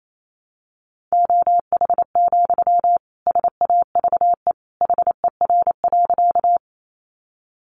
Morse code, heard as a keyed tone, about 28 wpm, 700 Hz. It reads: OH, SAVE HER.
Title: Sentences from Top 1100 Words 28wpm